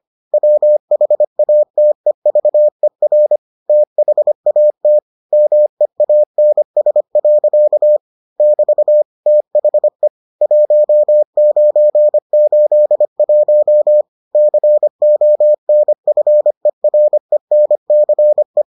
Morse code, heard as a keyed tone, about 25 wpm, 600 Hz. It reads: WHATEVER THAT MEANS. <BT> THE 1981 CONFERENCE